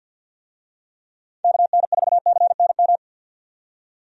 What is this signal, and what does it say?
Morse code, heard as a keyed tone, about 52 wpm, 700 Hz. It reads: KN4CNM
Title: USA Call Signs - 6 Characters - 50wpm